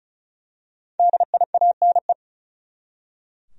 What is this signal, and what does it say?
Morse code, heard as a keyed tone, about 36 wpm, 700 Hz. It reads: DIANE
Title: USA Common Names - Max 5 Letters - 35wpm